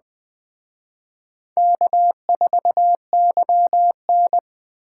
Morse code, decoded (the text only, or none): K4YN